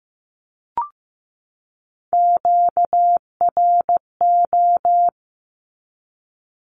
Morse code, heard as a keyed tone, about 15 wpm, 700 Hz. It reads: QRO
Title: Q-Codes 15wpm